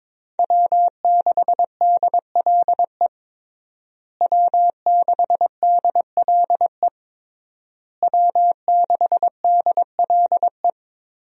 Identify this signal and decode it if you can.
Morse code, heard as a keyed tone, about 22 wpm, 700 Hz. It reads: W6DLE W6DLE W6DLE
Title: USA Call Signs - 5 Characters - Repeated 3x - 22wpm